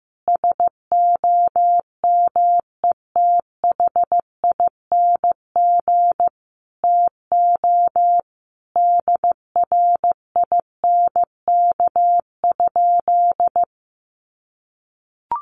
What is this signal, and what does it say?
Morse code, heard as a keyed tone, about 15 wpm, 700 Hz. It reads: SOMETHING TO DRINK?
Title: Sentences from Top 1500 Words 15wpm